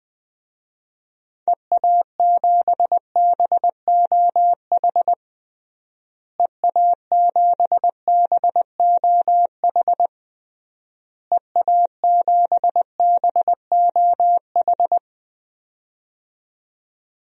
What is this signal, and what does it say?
Morse code, heard as a keyed tone, about 20 wpm, 700 Hz. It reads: EA7BOH EA7BOH EA7BOH
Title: DXCC-100 Call Signs - 6 Characters - Repeated 3x - 20wpm